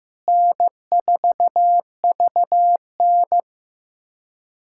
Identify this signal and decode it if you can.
Morse code, heard as a keyed tone, about 15 wpm, 700 Hz. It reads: N4VN
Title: USA Callsign DE Callsign - 4 Characters - 15wpm